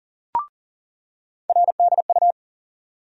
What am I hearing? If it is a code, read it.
Morse code, heard as a keyed tone, about 39 wpm, 700 Hz. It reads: RDU
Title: USA Call Sign Postfixes - 3 Letters 40wpm